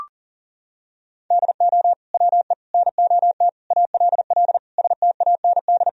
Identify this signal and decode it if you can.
Morse code, heard as a keyed tone, about 40 wpm, 700 Hz. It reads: DO WE NOT ALL STAND
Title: Sentences from Top 1000 Words - Review Part 3 - 40wpm